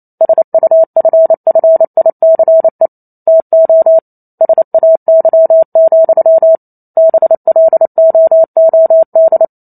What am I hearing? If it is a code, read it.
Morse code, heard as a keyed tone, about 28 wpm, 650 Hz. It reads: SUFFICE TO SAY, BLOOD